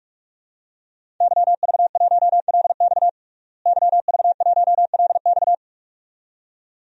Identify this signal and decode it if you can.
Morse code, heard as a keyed tone, about 47 wpm, 700 Hz. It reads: YV1LX YV1LX
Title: DXCC-100 Call Signs - 5 Characters - Repeated 2x - 45wpm